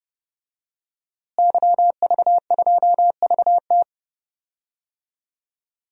Morse code, heard as a keyed tone, about 32 wpm, 700 Hz. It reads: YV2VT
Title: DXCC-100 Call Signs - 5 Characters - Repeated 3x - 30wpm